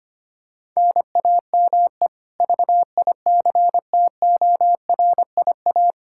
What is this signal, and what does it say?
Morse code, heard as a keyed tone, about 25 wpm, 700 Hz. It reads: NAME VICTORIA
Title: Practice QSO - Exchange 1 - Beginning - 25wpm